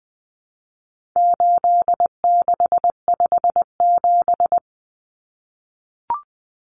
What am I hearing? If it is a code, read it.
Morse code, heard as a keyed tone, about 20 wpm, 700 Hz. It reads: 8657
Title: Common Cut Numbers - 4-Digit 20wpm